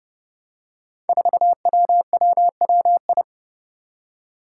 Morse code, heard as a keyed tone, about 30 wpm, 700 Hz. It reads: 4WWWI
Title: Five Challenging Letter-Number Combinations 30wpm